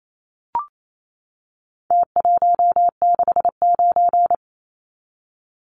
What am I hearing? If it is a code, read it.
Morse code, heard as a keyed tone, about 28 wpm, 700 Hz. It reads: T169
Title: Numbers - Contest Style - 4-Digit 28wpm